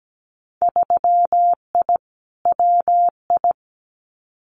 Morse code, heard as a keyed tone, about 17 wpm, 700 Hz. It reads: 3I WI
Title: Winter Field Day Exchanges 17wpm